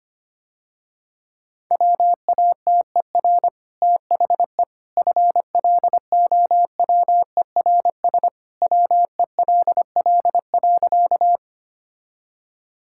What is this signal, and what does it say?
Morse code, heard as a keyed tone, about 25 wpm, 700 Hz. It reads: WATER THE FLOWERS WELL.
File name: Easy Sentences with Hints - 25wpm